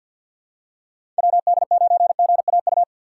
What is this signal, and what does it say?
Morse code, heard as a keyed tone, about 51 wpm, 700 Hz. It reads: WD9GAU
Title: USA Call Signs - 6 Characters - 4x Character Spacing - 50wpm